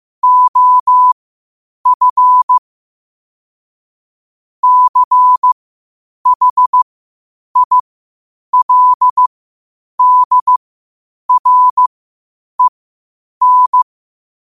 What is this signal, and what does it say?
Morse code, heard as a keyed tone, about 15 wpm, spaced as for 8 wpm, 1000 Hz. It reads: OF CHILDREN
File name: qotdfiles / 8wpm